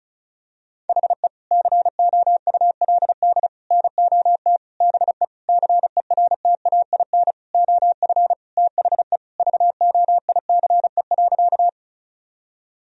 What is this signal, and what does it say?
Morse code, heard as a keyed tone, about 34 wpm, 700 Hz. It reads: HE COULD NOT BE CERTAIN OF THE VOICE.
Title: Sentences from Top 1000 Words - Review Part 2 - 35wpm